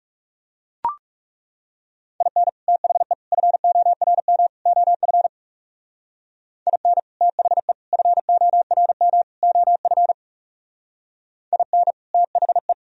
Morse code, heard as a keyed tone, about 36 wpm, 700 Hz. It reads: IN THE FORM OF IN THE FORM OF IN THE
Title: Speed-Racing - Sets of 4 Words 30wpm